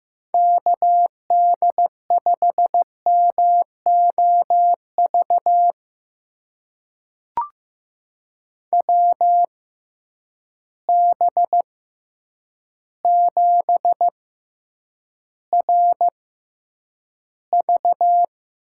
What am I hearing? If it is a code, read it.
Morse code, heard as a keyed tone, about 15 wpm, spaced as for 8 wpm, 700 Hz. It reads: KD5MOV WB7RV
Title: USA Call Signs - 6 Characters - 6x Character Spacing - 15wpm